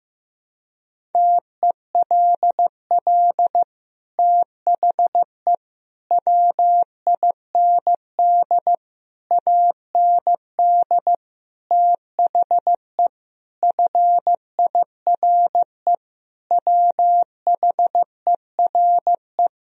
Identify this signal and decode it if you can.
Morse code, heard as a keyed tone, about 15 wpm, 700 Hz. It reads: TELL THE WIND AND THE FIRE WHERE
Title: Sentences from Top 1000 Words - Review Part 2 - 15wpm